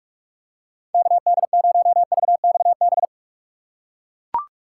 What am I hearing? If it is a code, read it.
Morse code, heard as a keyed tone, about 43 wpm, 700 Hz. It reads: KD0VXB
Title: USA Call Signs - 6 Characters - 3x Character Spacing - 45wpm